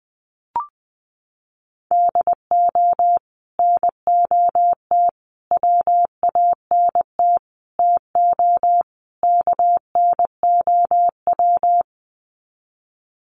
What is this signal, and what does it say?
Morse code, heard as a keyed tone, about 20 wpm, 700 Hz. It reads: DO NOT WANT TO KNOW
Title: Sets of 5 Words Encore 20wpm